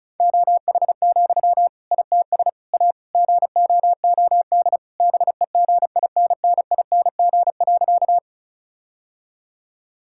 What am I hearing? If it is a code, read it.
Morse code, heard as a keyed tone, about 36 wpm, 700 Hz. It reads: OH, ITS A GOOD BEGINNING.
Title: Sentences from Top 1000 Words - Review Part 5 - 35wpm